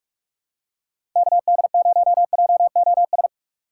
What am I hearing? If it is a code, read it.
Morse code, heard as a keyed tone, about 42 wpm, 700 Hz. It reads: KD0JOS
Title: USA Call Signs - 6 Characters - 4x Character Spacing - 45wpm